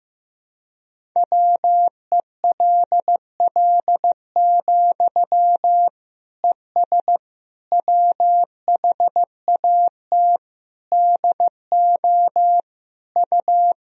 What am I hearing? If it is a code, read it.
Morse code, heard as a keyed tone, about 15 wpm, 700 Hz. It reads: WELL, ES WHAT DO U
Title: Sentences from Top 200 Words and Abbreviations 15wpm